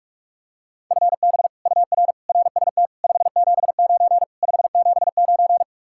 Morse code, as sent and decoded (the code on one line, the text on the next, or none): ..-. -... / ..- .-. / .-. ... - / ..... --... ----. / ..... --... ----.
FB UR RST 579 579